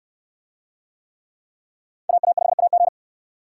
Morse code, heard as a keyed tone, about 69 wpm, 700 Hz. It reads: AA5AG